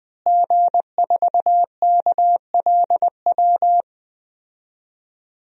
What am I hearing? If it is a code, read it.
Morse code, heard as a keyed tone, about 20 wpm, 700 Hz. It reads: G4KLW
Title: DXCC-100 Call Signs - 5 Characters - Repeated 3x - 20wpm